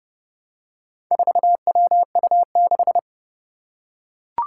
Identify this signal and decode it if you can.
Morse code, heard as a keyed tone, about 31 wpm, 700 Hz. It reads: 4WU6
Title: Four Challenging Letter-Number Combinations 30wpm